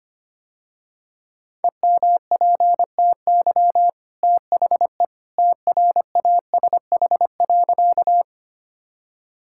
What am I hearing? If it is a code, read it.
Morse code, heard as a keyed tone, about 25 wpm, 700 Hz. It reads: EMPTY THE TRASH.
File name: Easy Sentences with Hints - 25wpm